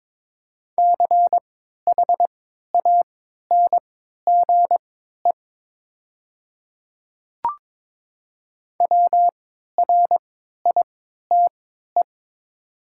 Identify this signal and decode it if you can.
Morse code, heard as a keyed tone, about 22 wpm, spaced as for 12 wpm, 700 Hz. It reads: CHANGE WRITE
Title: 3x Character Spacing - Top 200 Words - 22wpm